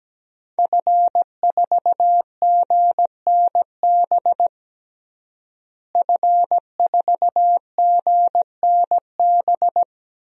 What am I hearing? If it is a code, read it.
Morse code, heard as a keyed tone, about 17 wpm, 700 Hz. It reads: F4GNB F4GNB
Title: DXCC-100 Call Signs - 5 Characters - Repeated 2x - 17wpm